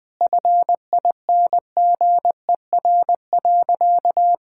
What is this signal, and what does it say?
Morse code, heard as a keyed tone, about 21 wpm, 700 Hz. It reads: FINGER.
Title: Sentences from Top 1600 Words 20wpm